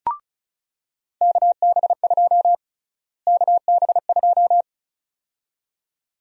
Morse code, heard as a keyed tone, about 36 wpm, 700 Hz. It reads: KB2 KB2
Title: USA Call Sign Prefixes - Repeated 2x - 35wpm